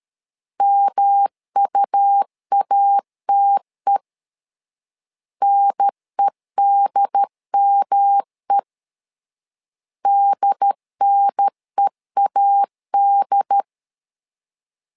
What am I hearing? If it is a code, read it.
Morse code, heard as a keyed tone, about 13 wpm, 800 Hz. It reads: MUATE NEDME DNEAD
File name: LEZIONI / 3 MWRO / 3 M 12wpm 1541